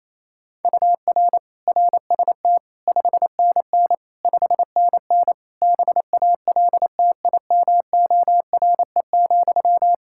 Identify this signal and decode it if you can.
Morse code, heard as a keyed tone, about 28 wpm, 700 Hz. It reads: UR RST 5NN 5NN BALTIMORE,
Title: Practice QSO - Exchange 1 - Beginning - 28wpm